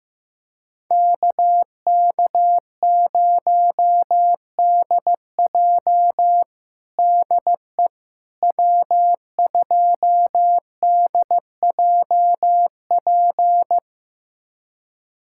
Morse code, heard as a keyed tone, about 15 wpm, 700 Hz. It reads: KK0DJ DE W2DJP
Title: USA Callsign DE Callsign - 5 Characters - 15wpm